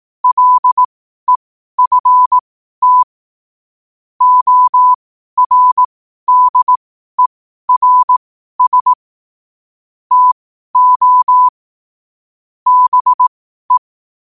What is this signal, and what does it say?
Morse code, heard as a keyed tone, about 18 wpm, spaced as for 12 wpm, 1000 Hz. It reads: LEFT ORDERS TO BE